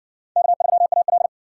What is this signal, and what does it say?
Morse code, heard as a keyed tone, about 60 wpm, 700 Hz. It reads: K3AL